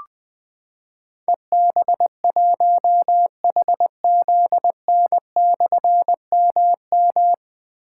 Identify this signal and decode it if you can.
Morse code, heard as a keyed tone, about 20 wpm, 700 Hz. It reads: EB1HZN/MM
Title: Difficult Call Signs 20wpm